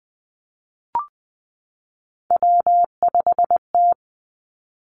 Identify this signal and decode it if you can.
Morse code, heard as a keyed tone, about 20 wpm, 700 Hz. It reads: W5T
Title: Three Challenging Letter-Number Combinations 20wpm